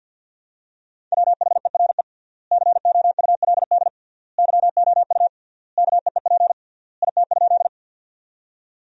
Morse code, heard as a keyed tone, about 50 wpm, 700 Hz. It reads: WHERE COULD YOU KEEP IT?
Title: 3x Word-Spacing - Sentences from Top 300 Words - 50wpm